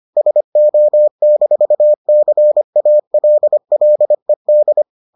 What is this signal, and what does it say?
Morse code, heard as a keyed tone, about 25 wpm, 600 Hz. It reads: SO-CALLED